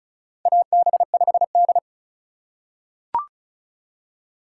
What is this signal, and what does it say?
Morse code, heard as a keyed tone, about 34 wpm, 700 Hz. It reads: AB5D
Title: USA Call Signs - 3x Character Spacing - 35wpm